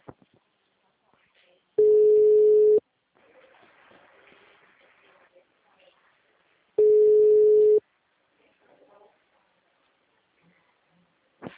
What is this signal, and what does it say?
Ringback tone (Europe, 425 Hz).